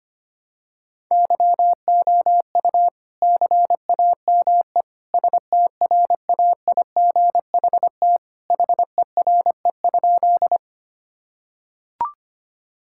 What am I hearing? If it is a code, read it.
Morse code, heard as a keyed tone, about 25 wpm, 700 Hz. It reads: YOU CAME STRAIGHT HERE?
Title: Sentences from Top 1000 Words - Review Part 6 - 25wpm